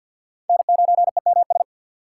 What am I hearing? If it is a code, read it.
Morse code, heard as a keyed tone, about 52 wpm, 700 Hz. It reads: N9EMS
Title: USA Call Signs - 5 Characters - 2x Character Spacing - 50wpm